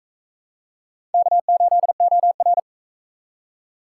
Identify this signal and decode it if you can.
Morse code, heard as a keyed tone, about 43 wpm, 700 Hz. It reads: K8OR